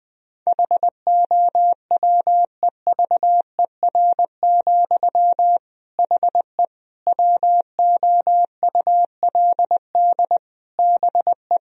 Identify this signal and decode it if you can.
Morse code, heard as a keyed tone, about 21 wpm, 700 Hz. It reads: HOWEVER, HE WOULD BE
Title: Sentences from Top 1000 Words - Review Part 2 - 20wpm